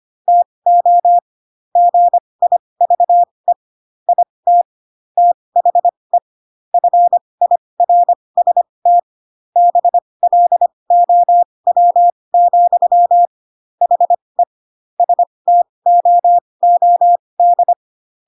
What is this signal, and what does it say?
Morse code, heard as a keyed tone, about 25 wpm, spaced as for 20 wpm, 700 Hz. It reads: TO GIVE IT THE FIRST BLOW, HE STOOD
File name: Pinocchio / Pinocchio ch1 15 25 ik1hge